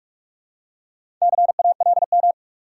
Morse code, heard as a keyed tone, about 44 wpm, 700 Hz. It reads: CALM
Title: Four Letter Words - 45wpm